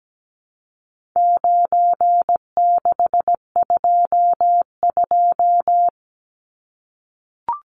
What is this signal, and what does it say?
Morse code, heard as a keyed tone, about 17 wpm, 700 Hz. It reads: 9622
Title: Numbers - 4-Digit 17wpm